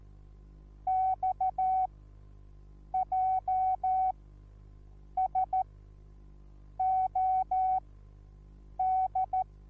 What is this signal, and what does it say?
Morse code, heard as a keyed tone, about 14 wpm, spaced as for 6 wpm, 750 Hz. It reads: XJSOD